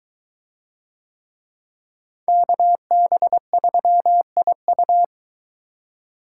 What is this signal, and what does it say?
Morse code, heard as a keyed tone, about 24 wpm, 700 Hz. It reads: KB3IU